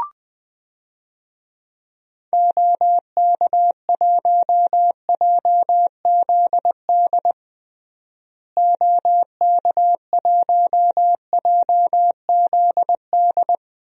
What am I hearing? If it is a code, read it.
Morse code, heard as a keyed tone, about 20 wpm, 700 Hz. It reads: OK1JZD OK1JZD
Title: DXCC-100 Call Signs - 6 Characters - Repeated 2x - 20wpm